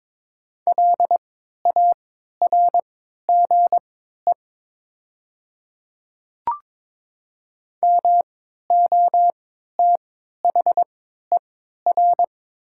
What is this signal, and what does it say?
Morse code, heard as a keyed tone, about 22 wpm, spaced as for 13 wpm, 700 Hz. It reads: LARGE MOTHER
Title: 3x Character Spacing - Top 200 Words - 22wpm